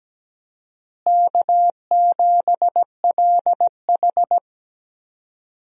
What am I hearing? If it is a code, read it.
Morse code, heard as a keyed tone, about 17 wpm, 700 Hz. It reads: K7LH
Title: USA Call Signs - 4 Characters - Repeated 2x - 17wpm